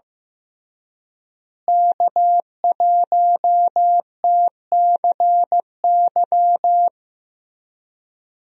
Morse code, heard as a keyed tone, about 16 wpm, 700 Hz. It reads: K1TCY
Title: USA Call Signs - 5 Characters - Repeated 3x - 15wpm